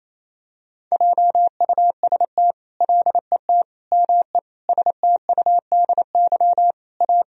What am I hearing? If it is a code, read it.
Morse code, heard as a keyed tone, about 28 wpm, 700 Hz. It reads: JUST LET ME STUDY A
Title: Sentences from Top 1000 Words - Review Part 1 - 28wpm